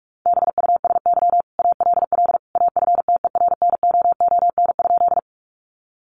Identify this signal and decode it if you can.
Morse code, heard as a keyed tone, about 44 wpm, 700 Hz. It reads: BUSY ALL AFTERNOON?
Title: Sentences from Top 1500 Words 45wpm